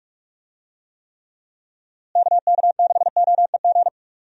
Morse code, heard as a keyed tone, about 42 wpm, 700 Hz. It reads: KK6OEG
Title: USA Call Signs - 6 Characters - Repeated 3x - 45wpm